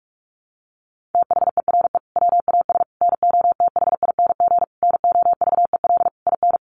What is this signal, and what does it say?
Morse code, heard as a keyed tone, about 44 wpm, 700 Hz. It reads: THERE WAS NOTHING NOVEL IN
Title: Sentences from Top 1300 Words 45wpm